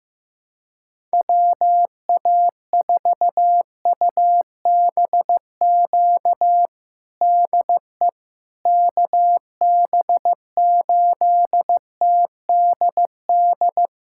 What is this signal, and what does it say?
Morse code, heard as a keyed tone, about 16 wpm, 700 Hz. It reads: WA4UBQ DE KB8TDD
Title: USA Callsign DE Callsign - 6 Characters - 15wpm